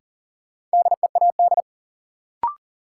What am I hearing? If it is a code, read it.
Morse code, heard as a keyed tone, about 40 wpm, 700 Hz. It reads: DEAD
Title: Sentences from Top 1500 Words - New Word Drill - 40wpm